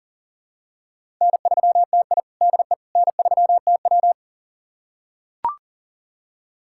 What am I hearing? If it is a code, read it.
Morse code, heard as a keyed tone, about 40 wpm, 700 Hz. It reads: N3TI DE N3TW
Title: USA Callsign DE Callsign - 4 Characters - 40wpm